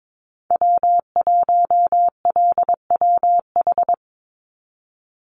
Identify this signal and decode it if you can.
Morse code, heard as a keyed tone, about 22 wpm, 700 Hz. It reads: W1LWH
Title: Call Signs Repeated 3x - 22wpm